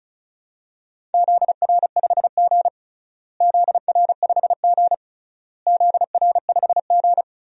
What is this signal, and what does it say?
Morse code, heard as a keyed tone, about 36 wpm, 700 Hz. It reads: ZR5G ZR5G ZR5G
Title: DXCC-100 Call Signs - 4 Characters - Repeated 3x - 35wpm